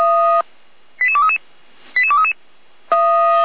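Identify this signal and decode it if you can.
Five-tone selective call (ZVEI), heard as quick runs of five tones, tones of about 70 ms. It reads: 88233 88233
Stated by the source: ZVEI five-tone selective call received on an SDR.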